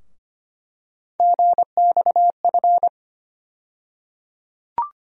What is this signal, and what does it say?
Morse code, heard as a keyed tone, about 25 wpm, 700 Hz. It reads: GXF
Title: USA Call Sign Postfixes - 3 Letters 25wpm